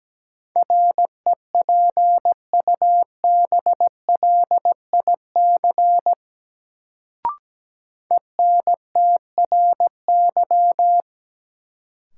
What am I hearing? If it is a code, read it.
Morse code, heard as a keyed tone, about 17 wpm, 700 Hz. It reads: REPUBLIC ENTRY
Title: Sentences from Top 1500 Words - New Word Drill Alt - 17wpm